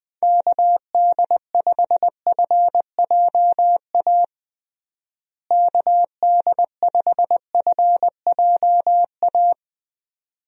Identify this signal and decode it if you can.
Morse code, heard as a keyed tone, about 20 wpm, 700 Hz. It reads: KD5FJA KD5FJA
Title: USA Call Signs - 6 Characters - Repeated 2x - 20wpm